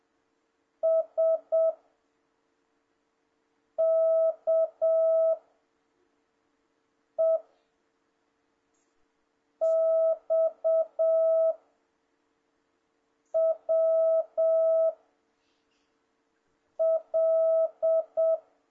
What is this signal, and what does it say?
Morse code, heard as a keyed tone, about 7 wpm, 650 Hz. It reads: S K E X W L
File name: jh8cbh / 2026cw